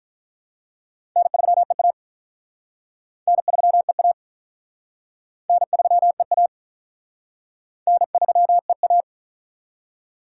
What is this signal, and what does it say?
Morse code, heard as a keyed tone, about 45 wpm, 700 Hz. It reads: N3EA N3EA N3EA N3EA